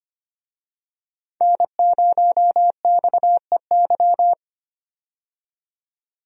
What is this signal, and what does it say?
Morse code, heard as a keyed tone, about 25 wpm, 700 Hz. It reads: N0XEY
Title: USA Call Signs - 5 Characters - 25wpm